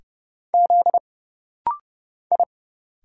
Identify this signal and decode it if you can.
Morse code, heard as a keyed tone, about 30 wpm, 700 Hz. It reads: Z I
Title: Single Letters - Rapid-Fire 30wpm